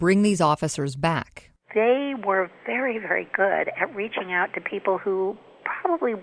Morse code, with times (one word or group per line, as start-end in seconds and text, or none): none